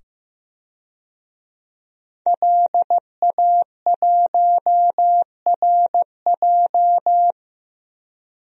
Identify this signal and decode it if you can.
Morse code, heard as a keyed tone, about 15 wpm, 700 Hz. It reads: LA1RJ
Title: DXCC-100 Call Signs - 5 Characters - 15wpm